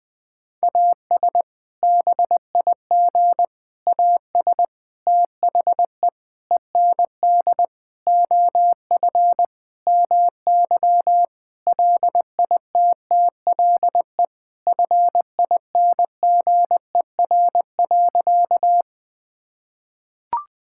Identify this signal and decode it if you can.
Morse code, heard as a keyed tone, about 20 wpm, 700 Hz. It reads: AS BIG AS THE END OF MY LITTLE FINGER.
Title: Sentences from Top 1600 Words 20wpm